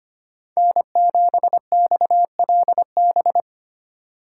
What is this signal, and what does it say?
Morse code, heard as a keyed tone, about 25 wpm, 700 Hz. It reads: N7XLB
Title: USA Call Signs - 5 Characters - 3x Character Spacing - 25wpm